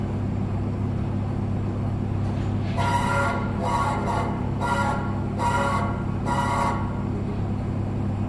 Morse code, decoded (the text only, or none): none